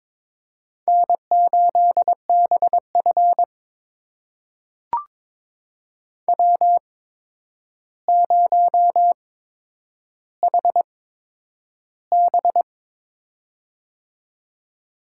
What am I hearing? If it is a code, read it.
Morse code, heard as a keyed tone, about 22 wpm, spaced as for 9 wpm, 700 Hz. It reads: N8BF W0HB
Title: USA Call Signs - 4 Characters - 8x Character Spacing - 22wpm